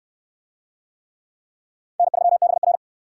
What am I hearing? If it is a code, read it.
Morse code, heard as a keyed tone, about 69 wpm, 700 Hz. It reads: N3BR